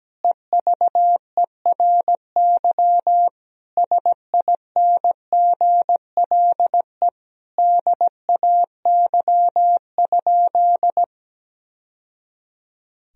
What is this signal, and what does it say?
Morse code, heard as a keyed tone, about 17 wpm, 700 Hz. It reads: EVERY SINGLE DAY?
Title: Sentences from Top 1000 Words - Review Part 4 - 17wpm